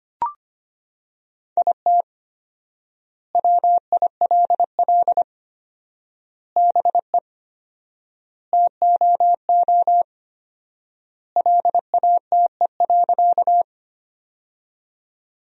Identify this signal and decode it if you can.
Morse code, heard as a keyed tone, about 25 wpm, 700 Hz. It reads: IT WILL BE TOO LATE.
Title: 4x Word-Spacing - Sentences from Top 300 Words - 25wpm